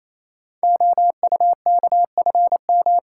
Morse code, heard as a keyed tone, about 29 wpm, 700 Hz. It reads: OUKFM
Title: Code Groups - 5 Letters - 28wpm